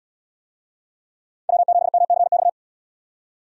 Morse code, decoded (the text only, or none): W7MCX